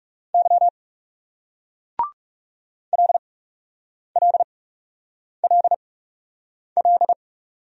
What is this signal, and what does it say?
Morse code, heard as a keyed tone, about 39 wpm, spaced as for 8 wpm, 700 Hz. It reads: Y LLLL